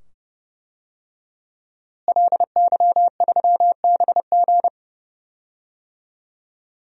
Morse code, heard as a keyed tone, about 28 wpm, 700 Hz. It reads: LY3BG
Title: DXCC-100 Call Signs - 5 Characters - Repeated 2x - 30wpm